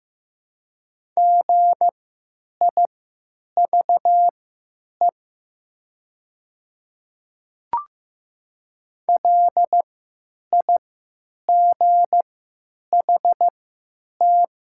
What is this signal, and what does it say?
Morse code, heard as a keyed tone, about 15 wpm, spaced as for 9 wpm, 700 Hz. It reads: GIVE LIGHT
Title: 3x Character Spacing - Top 300 Words - 15wpm